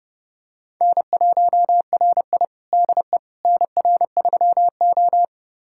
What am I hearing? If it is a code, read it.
Morse code, heard as a keyed tone, about 32 wpm, 700 Hz. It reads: N1RI DE NR3O
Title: USA Callsign DE Callsign - 4 Characters - 30wpm